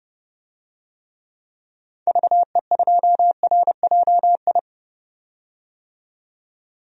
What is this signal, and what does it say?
Morse code, heard as a keyed tone, about 31 wpm, 700 Hz. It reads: VE2RJI
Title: DXCC-100 Call Signs - 6 Characters - 30wpm